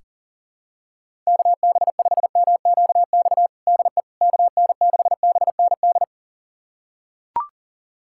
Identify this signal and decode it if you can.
Morse code, heard as a keyed tone, about 37 wpm, 700 Hz. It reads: KB5MQX DE KN6BND
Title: USA Callsign DE Callsign 40wpm